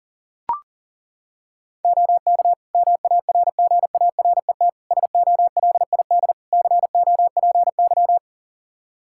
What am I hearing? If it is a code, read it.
Morse code, heard as a keyed tone, about 40 wpm, 700 Hz. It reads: OK MARGARET SOLID COPY
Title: Practice QSO - Exchange 3 - More Info - 40wpm